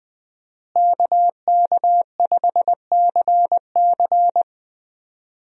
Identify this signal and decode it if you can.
Morse code, heard as a keyed tone, about 20 wpm, 700 Hz. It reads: KK5CC
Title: USA Call Signs - 5 Characters - Repeated 3x - 20wpm